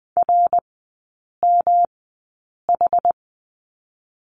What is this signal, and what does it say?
Morse code, heard as a keyed tone, about 20 wpm, spaced as for 8 wpm, 700 Hz. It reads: RMH
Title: 107 Lesson 18e - Letter m - Three Characters 20wpm